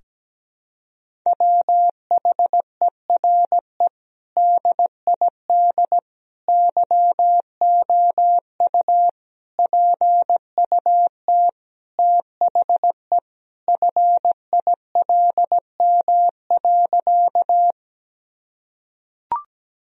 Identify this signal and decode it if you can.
Morse code, heard as a keyed tone, about 17 wpm, 700 Hz. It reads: WHERE DID YOU PUT THE FILM.
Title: Sentences from Top 1000 Words - Review Part 3 - 17wpm